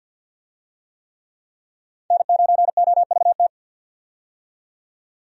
Morse code, heard as a keyed tone, about 49 wpm, 700 Hz. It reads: N9OVT
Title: USA Call Signs - 5 Characters - 50wpm